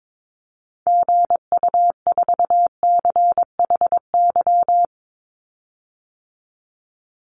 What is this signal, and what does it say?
Morse code, heard as a keyed tone, about 22 wpm, 700 Hz. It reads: GU4CHY
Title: Call Signs - 22wpm